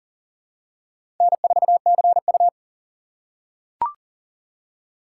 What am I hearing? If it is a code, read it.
Morse code, heard as a keyed tone, about 40 wpm, 690 Hz. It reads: N4CU
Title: USA Call Signs - 4 Characters - 2x Character Spacing - 40wpm